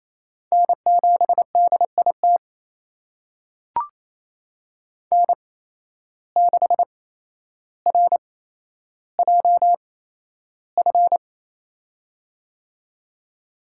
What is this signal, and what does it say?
Morse code, heard as a keyed tone, about 28 wpm, spaced as for 12 wpm, 700 Hz. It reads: N7DIT N6RJF
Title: USA Call Signs - 5 Characters - 8x Character Spacing - 28wpm